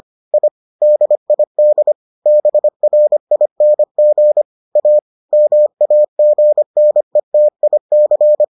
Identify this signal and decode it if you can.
Morse code, heard as a keyed tone, about 25 wpm, 600 Hz. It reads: I DID BRING A MAGNETIC